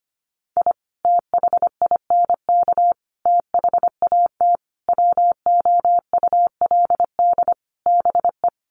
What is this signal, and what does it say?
Morse code, heard as a keyed tone, about 25 wpm, 700 Hz. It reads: I THINK THAT WOULD BE
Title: Sets of 5 Words Encore2 25wpm